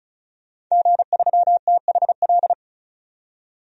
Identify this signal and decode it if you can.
Morse code, heard as a keyed tone, about 34 wpm, 700 Hz. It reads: G3THL
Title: DXCC-100 Call Signs - 5 Characters - 35wpm